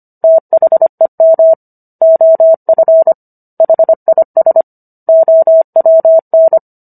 Morse code, heard as a keyed tone, about 25 wpm, 650 Hz. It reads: THEM OF HIS OWN